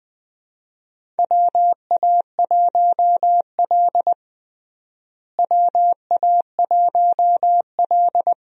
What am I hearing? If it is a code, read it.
Morse code, heard as a keyed tone, about 20 wpm, 700 Hz. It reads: WA1L WA1L
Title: USA Call Signs - 4 Characters - Repeated 2x - 20wpm